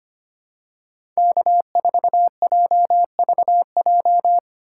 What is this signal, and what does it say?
Morse code, heard as a keyed tone, about 25 wpm, 700 Hz. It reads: K4JVJ
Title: USA Call Signs - 5 Characters - 8x Character Spacing - 25wpm